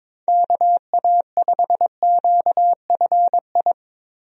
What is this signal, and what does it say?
Morse code, heard as a keyed tone, about 22 wpm, 700 Hz. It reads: KA5QFI
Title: USA Call Signs - 6 Characters - Repeated 3x - 22wpm